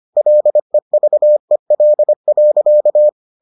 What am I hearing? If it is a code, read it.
Morse code, heard as a keyed tone, about 25 wpm, 600 Hz. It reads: LEVEL.